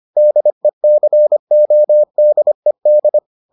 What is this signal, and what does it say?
Morse code, heard as a keyed tone, about 25 wpm, 600 Hz. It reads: DECODED